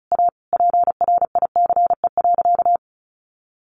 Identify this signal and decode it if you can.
Morse code, heard as a keyed tone, about 34 wpm, 700 Hz. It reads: A PRICE.
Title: Sentences from Top 1100 Words 35wpm